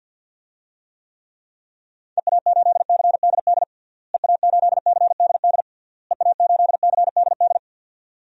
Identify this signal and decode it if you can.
Morse code, heard as a keyed tone, about 47 wpm, 700 Hz. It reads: EA8CDD EA8CDD EA8CDD
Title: DXCC-100 Call Signs - 6 Characters - Repeated 3x - 50wpm